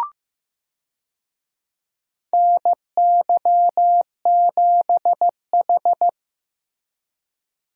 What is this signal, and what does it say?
Morse code, heard as a keyed tone, about 16 wpm, 700 Hz. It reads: NY7H
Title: USA Call Signs - 4 Characters - Repeated 2x - 15wpm